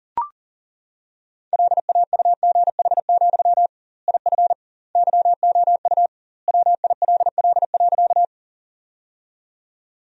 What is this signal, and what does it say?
Morse code, heard as a keyed tone, about 37 wpm, 700 Hz. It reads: LAUGH, IF YOU WILL.
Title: 2x Word-Spacing - Sentences from Top 500 Words - 40wpm